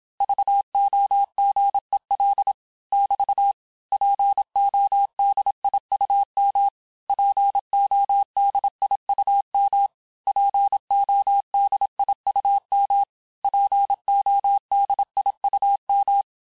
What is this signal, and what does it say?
Morse code, heard as a keyed tone, about 27 wpm, 800 Hz. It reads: VOGEL <BT> PODIUM PODIUM PODIUM PODIUM
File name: cours-du-mercredi / audios / cours-869-texte-LES-MOTS-vitesse-25-MPM